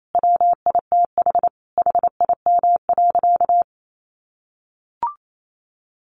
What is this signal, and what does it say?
Morse code, heard as a keyed tone, about 28 wpm, 700 Hz. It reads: WITH HIM.
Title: Sentences from Top 600 Words 28wpm